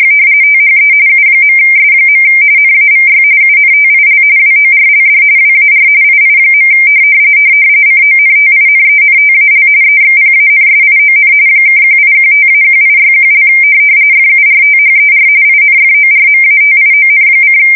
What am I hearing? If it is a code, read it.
RTTY, about 45.45 baud. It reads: THE U.S. NAVY UNTIL WORLD WAR II. CONTINENTAL EXPANSION (1815-1861) FURTHER INFORMATION:
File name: fsk170 / usnavy2